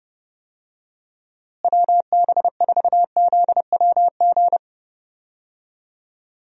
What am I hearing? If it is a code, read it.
Morse code, heard as a keyed tone, about 30 wpm, 700 Hz. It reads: WB4ZWG
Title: USA Call Signs - 6 Characters - Repeated 3x - 30wpm